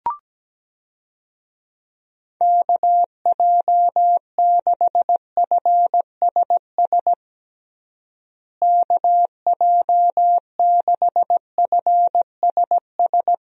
Morse code, heard as a keyed tone, about 17 wpm, 700 Hz. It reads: KJ6FSS KJ6FSS
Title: USA Call Signs - 6 Characters - Repeated 2x - 17wpm